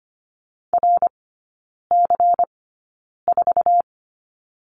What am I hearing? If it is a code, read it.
Morse code, heard as a keyed tone, about 25 wpm, spaced as for 9 wpm, 700 Hz. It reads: RC4
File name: course / 25:10wpm / 093 Lesson 16e - Letter c - Three Characters 25wpm